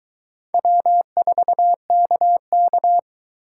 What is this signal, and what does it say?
Morse code, heard as a keyed tone, about 22 wpm, 700 Hz. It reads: W4KK